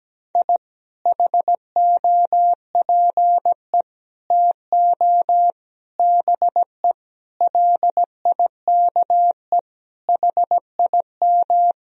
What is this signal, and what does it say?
Morse code, heard as a keyed tone, about 17 wpm, 700 Hz. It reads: I HOPE TO BE LIKE HIM
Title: Sentences from Top 1000 Words - Review Part 1 - 17wpm